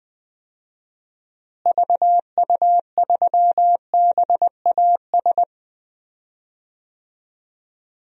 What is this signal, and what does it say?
Morse code, heard as a keyed tone, about 20 wpm, 700 Hz. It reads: VU3BAS
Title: DXCC-100 Call Signs - 6 Characters - Repeated 2x - 20wpm